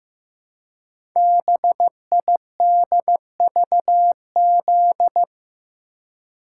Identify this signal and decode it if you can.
Morse code, heard as a keyed tone, about 15 wpm, 700 Hz. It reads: BIDVZ
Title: Code Groups - 5 Letter:Numbers - 15wpm